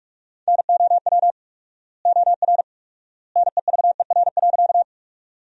Morse code, heard as a keyed tone, about 43 wpm, 700 Hz. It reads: NOW OR NEVER.
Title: 4x Word-Spacing - Sentences from Top 300 Words - 45wpm